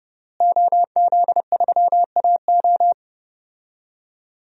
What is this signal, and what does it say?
Morse code, heard as a keyed tone, about 30 wpm, 700 Hz. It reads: OZ3AO
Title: DXCC-100 Call Signs - 5 Characters - Repeated 2x - 30wpm